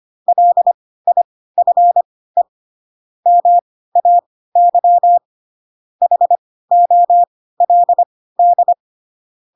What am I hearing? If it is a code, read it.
Morse code, heard as a keyed tone, about 25 wpm, spaced as for 16 wpm, 700 Hz. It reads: LIFE MAY HOLD
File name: Pinocchio / Pinocchio ch7 10 25 ik1hge